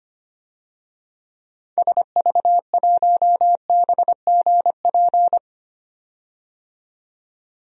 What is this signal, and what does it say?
Morse code, heard as a keyed tone, about 25 wpm, 690 Hz. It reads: SV1BGP
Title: DXCC-100 Call Signs - 6 Characters - 25wpm